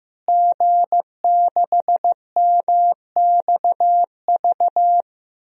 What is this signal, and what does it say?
Morse code, heard as a keyed tone, about 15 wpm, 700 Hz. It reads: G6MXV
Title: DXCC-100 Call Signs - 5 Characters - Repeated 3x - 15wpm